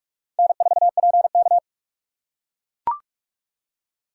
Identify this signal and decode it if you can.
Morse code, heard as a keyed tone, about 44 wpm, 700 Hz. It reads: N4PK